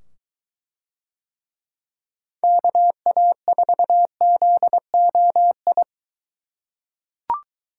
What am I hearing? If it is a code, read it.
Morse code, heard as a keyed tone, about 22 wpm, 700 Hz. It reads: KA4ZOI